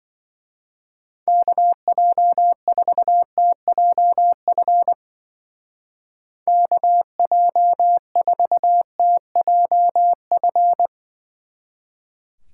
Morse code, heard as a keyed tone, about 21 wpm, 700 Hz. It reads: KJ4TJF KJ4TJF
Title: Speed Racing - USA Call Signs 20wpm